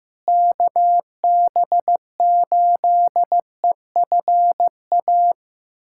Morse code, heard as a keyed tone, about 15 wpm, 700 Hz. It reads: KB8EFA
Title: USA Callsign DE Callsign - 6 Characters - 15wpm